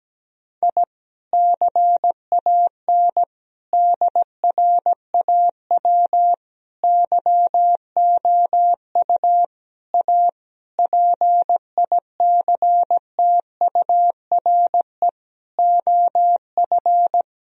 Morse code, heard as a keyed tone, about 17 wpm, 700 Hz. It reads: I CAN DRAW YOU A PICTURE OF
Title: Sentences from Top 1000 Words - Review Part 1 - 17wpm